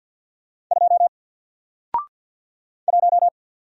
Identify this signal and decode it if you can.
Morse code, heard as a keyed tone, about 50 wpm, 700 Hz. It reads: 2 1
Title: Single Character (Letter:Number:Punctuation:Prosign) Alt 50wpm